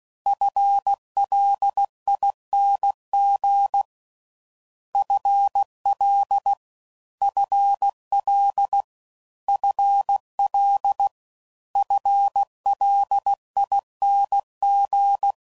Morse code, heard as a keyed tone, about 16 wpm, 800 Hz. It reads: FLING FL FL FL FLING